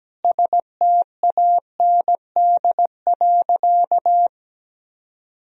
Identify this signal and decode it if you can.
Morse code, heard as a keyed tone, about 17 wpm, 700 Hz. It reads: STAND.
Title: Sentences from Top 1000 Words - Review Part 1 - 17wpm